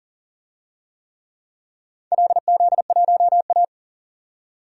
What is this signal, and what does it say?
Morse code, heard as a keyed tone, about 42 wpm, 700 Hz. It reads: LZ1A
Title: DXCC-100 Call Signs - 4 Characters - 40wpm